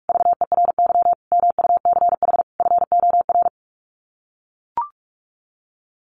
Morse code, heard as a keyed tone, about 44 wpm, 700 Hz. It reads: VERY MUCH FOR
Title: Sets of 4 Words 45wpm